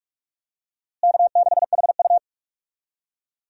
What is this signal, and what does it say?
Morse code, heard as a keyed tone, about 46 wpm, 700 Hz. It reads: K6HU
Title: USA Call Signs - 4 Characters - Repeated 3x - 45wpm